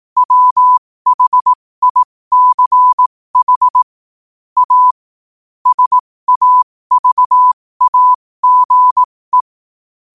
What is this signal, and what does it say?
Morse code, heard as a keyed tone, about 18 wpm, spaced as for 15 wpm, 1000 Hz. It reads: WHICH A SAVAGE
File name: qotdfiles / 15wpm